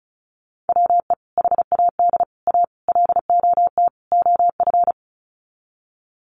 Morse code, heard as a keyed tone, about 34 wpm, 700 Hz. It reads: WE HAD A LOT OF
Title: Sets of 5 Words Encore 35wpm